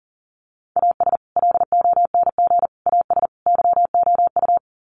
Morse code, heard as a keyed tone, about 40 wpm, 700 Hz. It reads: AS LONG AS YOU
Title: Sets of 4 Words 40wpm